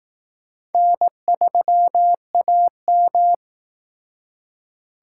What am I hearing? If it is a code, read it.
Morse code, heard as a keyed tone, about 19 wpm, 700 Hz. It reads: N3AM
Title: Speed Racing - USA 1x2 Call Signs 15wpm